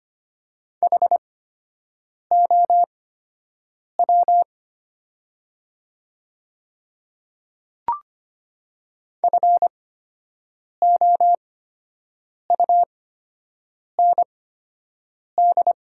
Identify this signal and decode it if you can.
Morse code, heard as a keyed tone, about 25 wpm, spaced as for 7 wpm, 700 Hz. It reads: HOW FOUND
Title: 8x Character Spacing - Top 300 Words - 25wpm